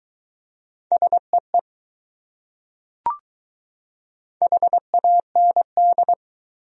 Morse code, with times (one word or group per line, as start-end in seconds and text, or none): 0.9-1.6 SEE
4.4-6.1 HAND